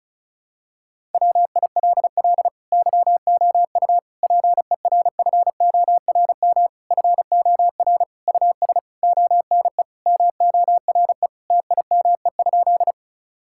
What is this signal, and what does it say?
Morse code, heard as a keyed tone, about 36 wpm, 700 Hz. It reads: WILL YOU PERFORM FOR US ONE MORE TIME?
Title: Sentences from Top 1000 Words - Review Part 5 - 35wpm